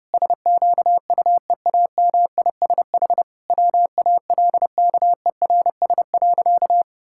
Morse code, heard as a keyed tone, about 32 wpm, 700 Hz. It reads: SQUEAMISH WALKERS.
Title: Pangram Challenge v2 - 30wpm